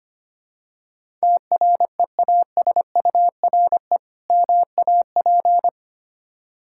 Morse code, heard as a keyed tone, about 25 wpm, 690 Hz. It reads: TREASURE MAP
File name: practice / 25wpm / Difficult Words with Hints - 25wpm